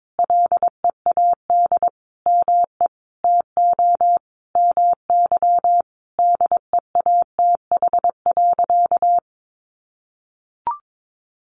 Morse code, heard as a keyed tone, about 22 wpm, 700 Hz. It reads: LEAD ME TO MY DEATH.
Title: Sentences from Top 1500 Words 22wpm